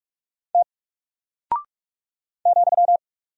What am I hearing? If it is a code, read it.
Morse code, heard as a keyed tone, about 44 wpm, 700 Hz. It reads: T ,
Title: Single Character (Letter:Number:Punctuation:Prosign) 45wpm